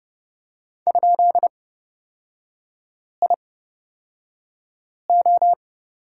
Morse code, heard as a keyed tone, about 29 wpm, spaced as for 5 wpm, 700 Hz. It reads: ?IO
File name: Lesson 36e - BT - Three Characters 30wpm